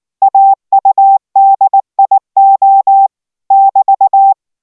Morse code, heard as a keyed tone, about 19 wpm, 780 Hz. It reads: AUDIO <BT>